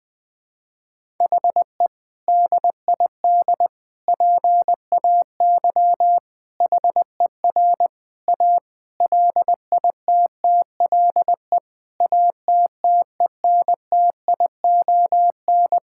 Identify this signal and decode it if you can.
Morse code, heard as a keyed tone, about 20 wpm, 700 Hz. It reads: HE DID PAY HER A LITTLE ATTENTION
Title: Sentences from Top 1000 Words - Review Part 6 - 20wpm